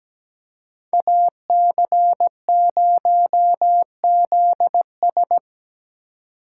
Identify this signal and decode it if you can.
Morse code, heard as a keyed tone, about 20 wpm, 700 Hz. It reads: AC0ZS
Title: Speed Racing - USA 2x2 Call Signs 17wpm